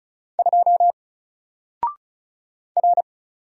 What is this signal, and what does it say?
Morse code, heard as a keyed tone, about 36 wpm, 700 Hz. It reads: 2 R
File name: Single-Letter-Number-35wpm